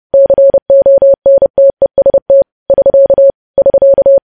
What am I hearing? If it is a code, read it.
Morse code, heard as a keyed tone, about 30 wpm, 550 Hz. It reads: CONTEST <SK> <SK>
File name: podcast / News headlines-1x-30x30